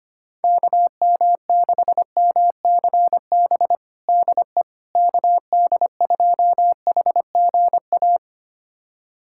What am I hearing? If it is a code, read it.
Morse code, heard as a keyed tone, about 25 wpm, 700 Hz. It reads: KM6MCB DE KD2HGA
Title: USA Callsign DE Callsign 25wpm